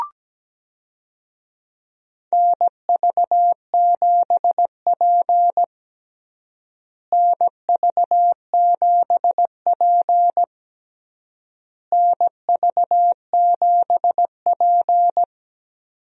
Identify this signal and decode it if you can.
Morse code, heard as a keyed tone, about 17 wpm, 700 Hz. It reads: NV7P NV7P NV7P
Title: USA Call Signs - 4 Characters - Repeated 3x - 17wpm